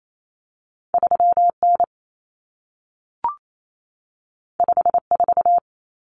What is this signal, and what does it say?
Morse code, heard as a keyed tone, about 28 wpm, 700 Hz. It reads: 3N 54
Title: Common Cut Numbers - 2-Digit 28wpm